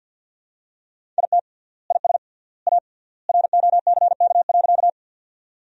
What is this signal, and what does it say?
Morse code, heard as a keyed tone, about 49 wpm, 700 Hz. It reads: IT IS A ROCK.
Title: 3x Word-Spacing - Sentences from Top 400 Words - 50wpm